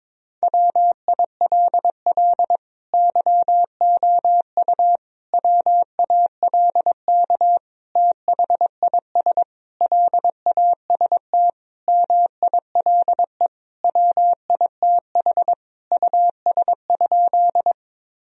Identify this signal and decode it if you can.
Morse code, heard as a keyed tone, about 22 wpm, 700 Hz. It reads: WILL YOU WALK THIS LAST MILE WITH US?
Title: Sentences from Top 1000 Words - Review Part 1 - 22wpm